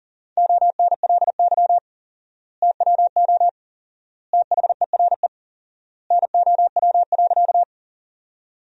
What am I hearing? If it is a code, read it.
Morse code, heard as a keyed tone, about 37 wpm, 700 Hz. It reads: ONLY TWO THERE NOW.
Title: 4x Word-Spacing - Sentences from Top 200 Words - 40wpm